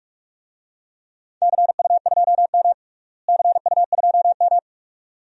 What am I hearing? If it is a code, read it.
Morse code, heard as a keyed tone, about 41 wpm, 700 Hz. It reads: CU2M CU2M
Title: DXCC-100 Call Signs - 4 Characters - Repeated 2x - 45wpm